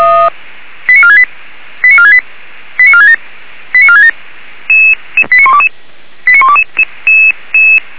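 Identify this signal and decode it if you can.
Five-tone selective call (ZVEI), heard as quick runs of five tones, tones of about 70 ms. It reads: 89468 89468 89467 89467 89122 89122
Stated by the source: ZVEI five-tone call 89372 received by SDR, Feuerwehr Orlamünde.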